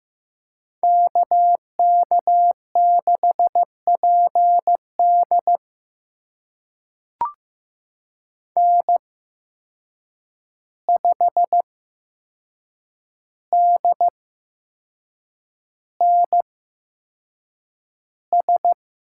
Morse code, heard as a keyed tone, about 15 wpm, spaced as for 6 wpm, 700 Hz. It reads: KK6PD N5DNS
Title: USA Call Signs - 5 Characters - 8x Character Spacing - 15wpm